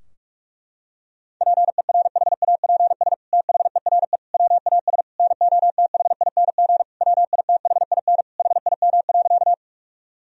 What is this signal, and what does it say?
Morse code, heard as a keyed tone, about 46 wpm, 700 Hz. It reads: PERHAPS THERE WAS NOTHING WITHIN HIM.
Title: Sentences from Top 1000 Words - Review Part 4 - 45wpm